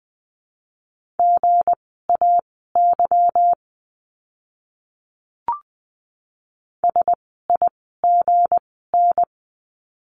Morse code, heard as a keyed tone, about 20 wpm, spaced as for 14 wpm, 700 Hz. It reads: GAY SIGN
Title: 160 Lesson 26f - Letter g - Words 20wpm